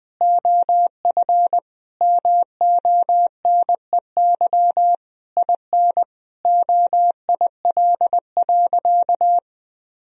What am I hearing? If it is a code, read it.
Morse code, heard as a keyed tone, about 20 wpm, 700 Hz. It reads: OF MONEY IN OIL.
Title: Sentences from Top 1000 Words - Review Part 4 - 20wpm